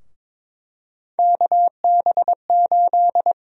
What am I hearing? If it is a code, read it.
Morse code, heard as a keyed tone, about 22 wpm, 700 Hz. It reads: KB8
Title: USA Call Sign Prefixes - 22wpm